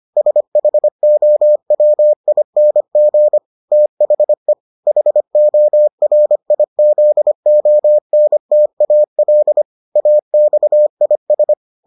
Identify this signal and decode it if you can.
Morse code, heard as a keyed tone, about 25 wpm, 600 Hz. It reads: SHOWING THE HORIZONTAL AXIS